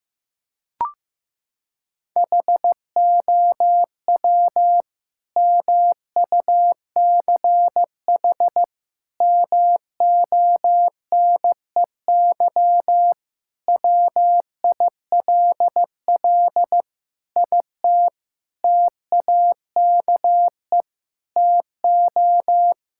Morse code, heard as a keyed tone, about 15 wpm, 700 Hz. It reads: HOW MUCH MONEY WILL IT TAKE TO
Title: Sentences from Top 1000 Words - Review Part 5 - 15wpm